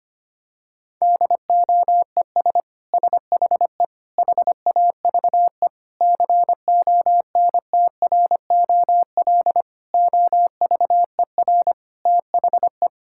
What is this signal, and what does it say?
Morse code, heard as a keyed tone, about 25 wpm, 700 Hz. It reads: DOES SHE HAVE CONTROL OVER THE
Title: Sentences from Top 1000 Words - Review Part 5 - 25wpm